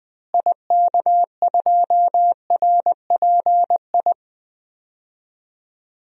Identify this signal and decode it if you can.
Morse code, heard as a keyed tone, about 20 wpm, 700 Hz. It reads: IK2RPI
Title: DXCC-100 Call Signs - 6 Characters - Repeated 2x - 20wpm